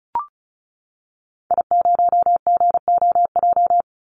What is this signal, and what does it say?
Morse code, heard as a keyed tone, about 34 wpm, 700 Hz. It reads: I0GOJ
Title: Call Signs - 35wpm